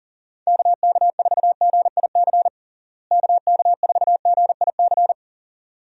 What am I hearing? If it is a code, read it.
Morse code, heard as a keyed tone, about 37 wpm, 700 Hz. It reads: KK4GIC KK4GIC
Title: USA Call Signs - 6 Characters - Repeated 2x - 40wpm